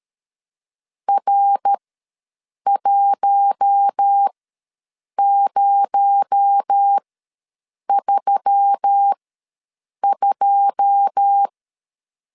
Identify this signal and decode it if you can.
Morse code, heard as a keyed tone, about 13 wpm, spaced as for 7 wpm, 800 Hz. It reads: R1032
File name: LEZIONI / 9 2 ws 12wpm 3406